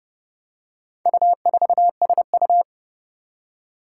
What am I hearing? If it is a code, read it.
Morse code, heard as a keyed tone, about 30 wpm, 700 Hz. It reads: U4SU
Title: Four Challenging Letter-Number Combinations 30wpm